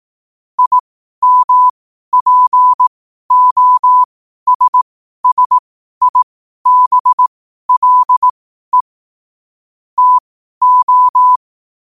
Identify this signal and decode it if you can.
Morse code, heard as a keyed tone, about 18 wpm, spaced as for 12 wpm, 1000 Hz. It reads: IMPOSSIBLE TO
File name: qotdfiles / 12wpm